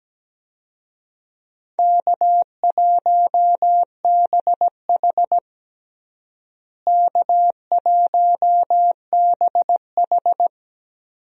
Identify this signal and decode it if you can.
Morse code, heard as a keyed tone, about 17 wpm, 700 Hz. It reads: K1BH K1BH
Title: USA Call Signs - 4 Characters - Repeated 3x - 17wpm